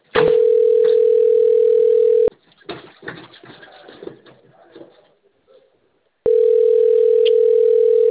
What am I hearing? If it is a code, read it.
Ringback tone (North America), 440 + 480 Hz.
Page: XV DE NOVEMBRO 2